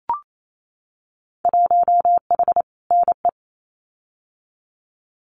Morse code, heard as a keyed tone, about 28 wpm, 700 Hz. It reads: 1H NE
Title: Winter Field Day Exchanges Encore 28wpm